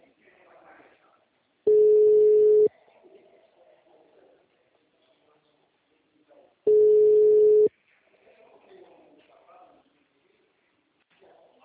Ringback tone (Europe, 425 Hz).
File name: out- -2006- - - .18764